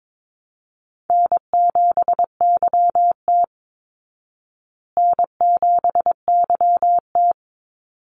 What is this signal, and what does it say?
Morse code, heard as a keyed tone, about 22 wpm, 700 Hz. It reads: N7YT N7YT
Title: Easy Call Signs Repeated 3x - 22wpm